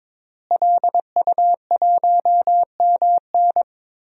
Morse code, heard as a keyed tone, about 22 wpm, 700 Hz. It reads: LU1MN
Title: DXCC-100 Call Signs - 5 Characters - Repeated 2x - 22wpm